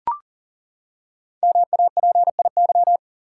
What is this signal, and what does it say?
Morse code, heard as a keyed tone, about 38 wpm, 700 Hz. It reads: MAPIY
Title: Code Groups - 5 Letters - 40wpm